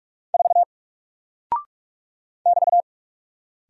Morse code, heard as a keyed tone, about 44 wpm, 700 Hz. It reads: 4 <BT>
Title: Single Character (Letter:Number:Punctuation:Prosign) 45wpm